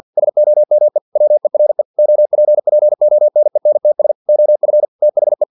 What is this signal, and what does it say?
Morse code, heard as a keyed tone, about 50 wpm, 600 Hz. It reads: SOME WERE OPPONENTS OF THE
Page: Never 49wpm